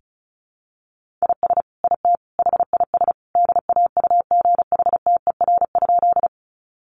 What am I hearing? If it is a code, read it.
Morse code, heard as a keyed tone, about 34 wpm, 700 Hz. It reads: IS IT HIS DAUGHTER?
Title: Sentences from Top 1500 Words 35wpm